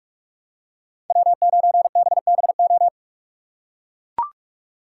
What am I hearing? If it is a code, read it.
Morse code, heard as a keyed tone, about 41 wpm, 700 Hz. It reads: W9BBO
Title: USA Call Signs - 5 Characters - 4x Character Spacing - 45wpm